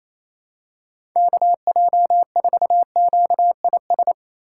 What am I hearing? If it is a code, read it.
Morse code, heard as a keyed tone, about 28 wpm, 700 Hz. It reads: KJ4QIS
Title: USA Call Signs - 6 Characters - 4x Character Spacing - 28wpm